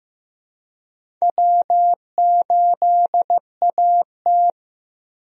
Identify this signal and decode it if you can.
Morse code, heard as a keyed tone, about 15 wpm, 700 Hz. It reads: W8AT
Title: USA Call Signs - 4 Characters - Repeated 2x - 15wpm